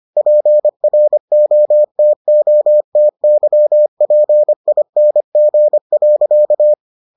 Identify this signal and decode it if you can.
Morse code, heard as a keyed tone, about 25 wpm, 600 Hz. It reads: PROTOTYPING.